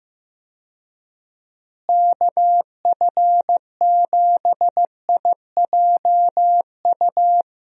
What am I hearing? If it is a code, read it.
Morse code, heard as a keyed tone, about 15 wpm, 700 Hz. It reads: KF7IJU
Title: USA Call Signs - 6 Characters - Repeated 3x - 15wpm